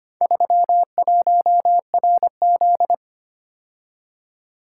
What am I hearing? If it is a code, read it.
Morse code, heard as a keyed tone, about 25 wpm, 700 Hz. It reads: 31RZ
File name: practice / 25wpm / Four Letter-Number Combinations - 25wpm